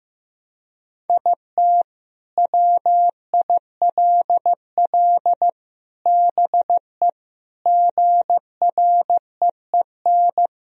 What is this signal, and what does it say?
Morse code, heard as a keyed tone, about 15 wpm, 700 Hz. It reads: IT WILL BE GREEN
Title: Sentences from Top 1000 Words - Review Part 2 - 15wpm